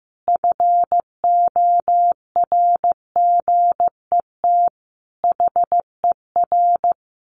Morse code, heard as a keyed tone, about 15 wpm, 700 Hz. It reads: FORGET HER
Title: Additional Easy Phrases 15wpm